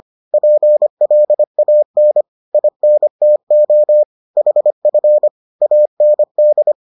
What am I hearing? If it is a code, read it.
Morse code, heard as a keyed tone, about 25 wpm, 600 Hz. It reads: PLAN INTO HF AND